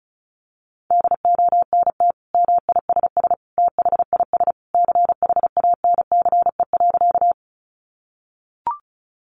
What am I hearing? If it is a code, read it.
Morse code, heard as a keyed tone, about 34 wpm, 700 Hz. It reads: DONT MISS THIS CHANCE.
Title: Sentences from Top 800 Words 35wpm